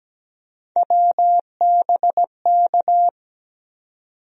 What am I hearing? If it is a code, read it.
Morse code, heard as a keyed tone, about 17 wpm, 700 Hz. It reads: WBK
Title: USA Call Sign Postfixes - 3 Letters 17wpm